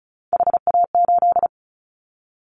Morse code, heard as a keyed tone, about 34 wpm, 700 Hz. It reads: HA8
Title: Callsign Prefixes for Top 25 DXCC - Repeated 3x - 35wpm